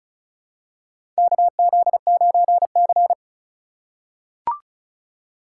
Morse code, heard as a keyed tone, about 36 wpm, 700 Hz. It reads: KZ9C
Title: USA Call Signs - 4 Characters - 4x Character Spacing - 35wpm